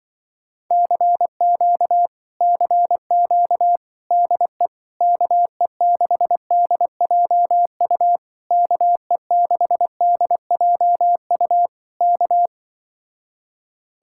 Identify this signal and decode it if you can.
Morse code, heard as a keyed tone, about 24 wpm, 700 Hz. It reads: CQ CQ DE KE6DJU KE6DJU K
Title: Speed-Racing Common QSO Phrases Encore - 20wpm